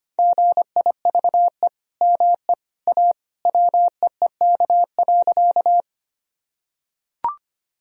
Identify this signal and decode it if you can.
Morse code, heard as a keyed tone, about 25 wpm, 700 Hz. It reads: GIVE ME A WEEK.
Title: Sentences from Top 1000 Words - Review Part 2 - 25wpm